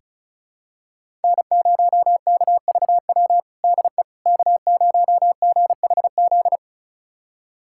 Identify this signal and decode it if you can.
Morse code, heard as a keyed tone, about 34 wpm, 700 Hz. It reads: N0KVW DE K0GHZ
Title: USA Callsign DE Callsign - 5 Characters - 35wpm